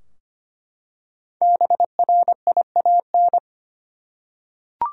Morse code, heard as a keyed tone, about 25 wpm, 700 Hz. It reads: BRIAN
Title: USA Common Names - Max 5 Letters - 25wpm